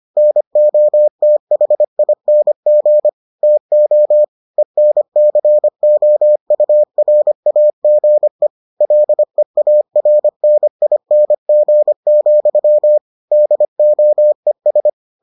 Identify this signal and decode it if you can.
Morse code, heard as a keyed tone, about 25 wpm, 600 Hz. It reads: NOTHING TO ENCOURAGE LEARNING, DOES